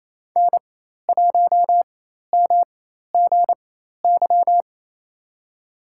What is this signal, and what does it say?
Morse code, heard as a keyed tone, about 28 wpm, spaced as for 13 wpm, 700 Hz. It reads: N1MGY
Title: USA Call Signs - 5 Characters - 4x Character Spacing - 28wpm